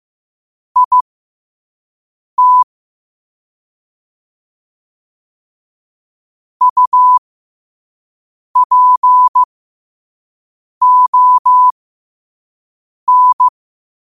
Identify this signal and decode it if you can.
Morse code, heard as a keyed tone, about 15 wpm, spaced as for 5 wpm, 1000 Hz. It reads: IT UPON
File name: qotdfiles / 5wpm